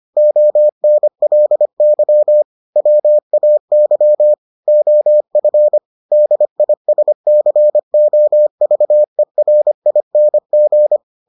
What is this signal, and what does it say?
Morse code, heard as a keyed tone, about 25 wpm, 600 Hz. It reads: ONLY WAY OF DISCOVERING